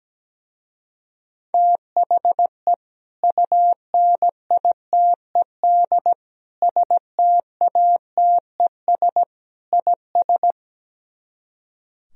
Morse code, heard as a keyed tone, about 17 wpm, 700 Hz. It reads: THE UNITED STATES IS
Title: Speed-Racing - Sets of 4 Words 17wpm